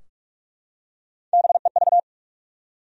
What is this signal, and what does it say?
Morse code, heard as a keyed tone, about 41 wpm, 700 Hz. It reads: BEV
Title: USA Common Names - Max 4 Letters - 45wpm